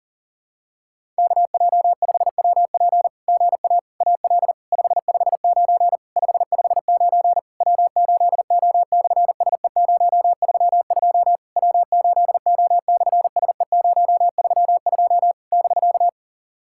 Morse code, heard as a keyed tone, about 37 wpm, 690 Hz. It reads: KJ5WP GA AL 559 559 W8O/SE032 W8O/SE032 <BK>
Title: SOTA QSO Practice 40wpm